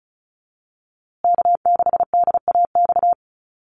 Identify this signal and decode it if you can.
Morse code, heard as a keyed tone, about 34 wpm, 700 Hz. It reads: K6DAX
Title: USA Call Signs Repeated 3x 35wpm